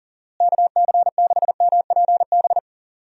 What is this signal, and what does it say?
Morse code, heard as a keyed tone, about 40 wpm, 700 Hz. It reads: KC6MPB
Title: USA Call Signs - 6 Characters - Repeated 3x - 40wpm